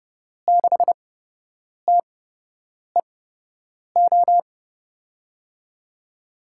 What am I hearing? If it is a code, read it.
Morse code, heard as a keyed tone, about 30 wpm, spaced as for 8 wpm, 700 Hz. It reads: 6TEO
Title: USA Call Signs - 6 Characters - 8x Character Spacing - 30wpm